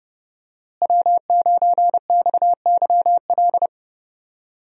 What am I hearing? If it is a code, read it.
Morse code, heard as a keyed tone, about 30 wpm, 700 Hz. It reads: W9XYL
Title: USA Call Signs - 5 Characters - Repeated 2x - 30wpm